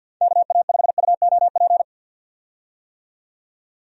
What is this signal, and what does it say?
Morse code, heard as a keyed tone, about 46 wpm, 700 Hz. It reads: KA5UOP
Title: USA Call Signs - 6 Characters - 50wpm